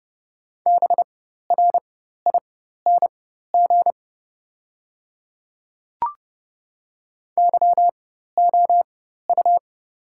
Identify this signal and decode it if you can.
Morse code, heard as a keyed tone, about 30 wpm, spaced as for 14 wpm, 700 Hz. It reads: BRING YOU
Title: 4x Character Spacing - Top 500 Words - 30wpm